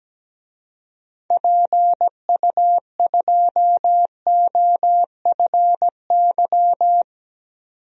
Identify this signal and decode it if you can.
Morse code, heard as a keyed tone, about 17 wpm, 700 Hz. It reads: PU2OFY
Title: DXCC-100 Call Signs - 6 Characters - 17wpm